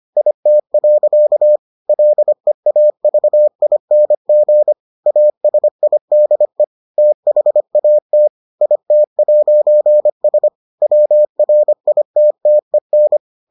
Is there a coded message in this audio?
Yes: Morse code from 0.2 to 13.2 s, heard as a keyed tone, about 25 wpm, 600 Hz. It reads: IT. LEAVING ASIDE THAT IT'S WRITTEN